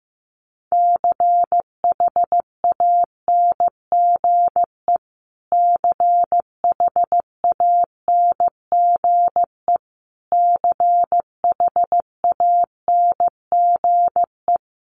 Morse code, heard as a keyed tone, about 14 wpm, 700 Hz. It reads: CHANGE CHANGE CHANGE
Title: Top 300 Words Repeated 15wpm